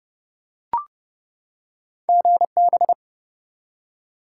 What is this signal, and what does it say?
Morse code, heard as a keyed tone, about 30 wpm, 700 Hz. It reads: GB
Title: Top 90 QSO Elements 30wpm